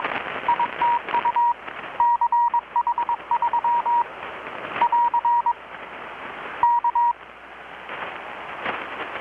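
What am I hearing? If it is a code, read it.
Morse code, heard as a keyed tone, about 21 wpm, 950 Hz. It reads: ITU CH3 <AR> K
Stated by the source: Morse from KPH's Night of Nights XX maritime radiotelegraphy event, received in Chiapas, Mexico about 3,800 km away on a Kenwood R-600 with a grounded 10 m long-wire antenna.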